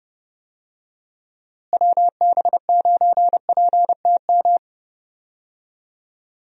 Morse code, heard as a keyed tone, about 30 wpm, 700 Hz. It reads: WB9PTM
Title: USA Call Signs - 6 Characters - Repeated 3x - 30wpm